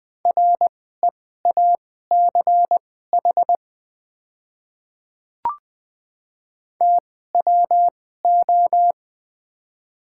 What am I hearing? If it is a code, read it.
Morse code, heard as a keyed tone, about 20 wpm, spaced as for 14 wpm, 700 Hz. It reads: REACH TWO
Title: 2x Character Spacing - Top 500 Words - 20wpm